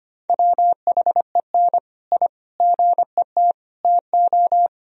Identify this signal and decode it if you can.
Morse code, heard as a keyed tone, about 25 wpm, 700 Hz. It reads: WHEN I GET TO
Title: Sentences from Top 1000 Words - Review Part 1 - 25wpm